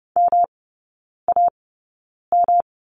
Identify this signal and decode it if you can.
Morse code, heard as a keyed tone, about 30 wpm, spaced as for 9 wpm, 700 Hz. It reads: MAM
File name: Lesson 19e - Letter w - Three Characters 30wpm